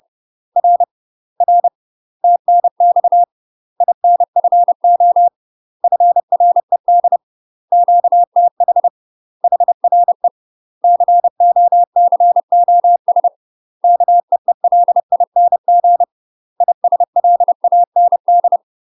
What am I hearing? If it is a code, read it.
Morse code, heard as a keyed tone, about 30 wpm, 700 Hz. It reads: R R TNX INFO FRED QTH HRE COCOS KEELING ISLAND
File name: UK MASTER QSOs TRAINING / 30WPM / UK MASTER QSOs 0000